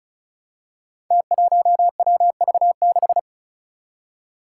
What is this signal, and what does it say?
Morse code, heard as a keyed tone, about 36 wpm, 700 Hz. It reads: T1WV6
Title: Five Challenging Letter-Number Combinations 35wpm